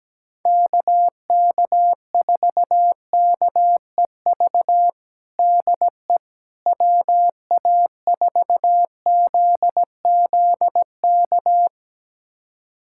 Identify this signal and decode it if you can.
Morse code, heard as a keyed tone, about 17 wpm, 700 Hz. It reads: KK4KEV DE WA4ZZK
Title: USA Callsign DE Callsign - 6 Characters - 17wpm